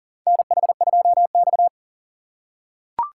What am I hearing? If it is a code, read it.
Morse code, heard as a keyed tone, about 40 wpm, 700 Hz. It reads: NH2X
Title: USA Call Signs - 4 Characters - 4x Character Spacing - 40wpm